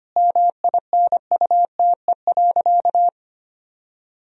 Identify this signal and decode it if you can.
Morse code, heard as a keyed tone, about 25 wpm, 700 Hz. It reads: MINUTE.
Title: Sentences from Top 1000 Words - Review Part 2 - 25wpm